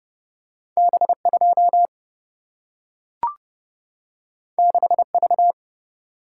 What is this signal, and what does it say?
Morse code, heard as a keyed tone, about 30 wpm, 700 Hz. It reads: B 2 6 V
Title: Single-Letter Single-Number 30wpm